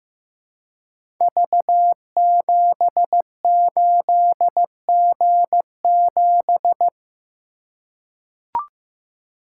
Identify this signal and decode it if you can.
Morse code, heard as a keyed tone, about 15 wpm, 700 Hz. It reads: V78G7
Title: Code Groups - 5 Letter:Numbers - 15wpm